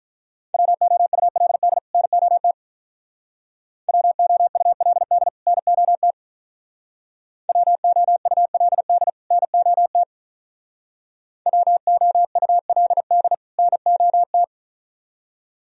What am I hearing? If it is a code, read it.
Morse code, heard as a keyed tone, about 44 wpm, 700 Hz. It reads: WOULD NOT WOULD NOT WOULD NOT WOULD NOT